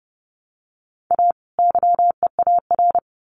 Morse code, heard as a keyed tone, about 30 wpm, 700 Hz. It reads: A YEAR
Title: Sets of 2 Words 30wpm